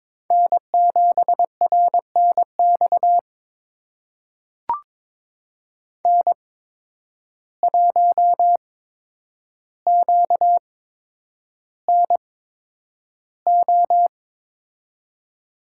Morse code, heard as a keyed tone, about 22 wpm, spaced as for 9 wpm, 700 Hz. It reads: N7RNX N1QNO
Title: USA Call Signs - 5 Characters - 8x Character Spacing - 22wpm